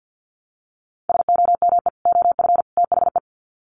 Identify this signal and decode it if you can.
Morse code, heard as a keyed tone, about 49 wpm, 700 Hz. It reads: SOME OF THE
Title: Sets of 3 Words 50wpm